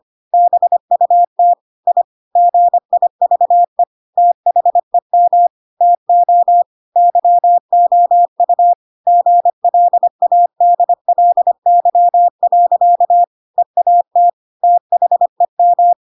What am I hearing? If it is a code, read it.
Morse code, heard as a keyed tone, about 25 wpm, 700 Hz. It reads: BUT I GIVE THEM TO YOU GLADLY. EAT THEM
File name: Pinocchio / Pinocchio ch7 25 25 ik1hge